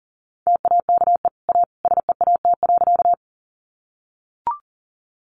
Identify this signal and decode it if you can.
Morse code, heard as a keyed tone, about 40 wpm, 700 Hz. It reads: TAKE A SEAT.
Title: Sentences from Top 1500 Words 40wpm